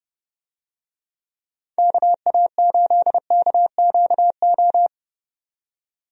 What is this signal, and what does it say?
Morse code, heard as a keyed tone, about 30 wpm, 700 Hz. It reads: KA8KQO
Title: USA Call Signs - 6 Characters - 30wpm